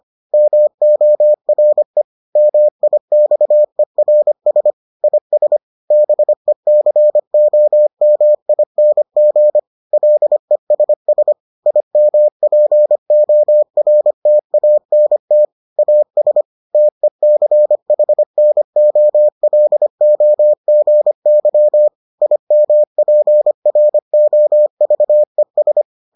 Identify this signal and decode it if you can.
Morse code, heard as a keyed tone, about 25 wpm, 600 Hz. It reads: MORE MIXERS IS BECOMING LESS IMPORTANT AS TECHNOLOGY IMPROVES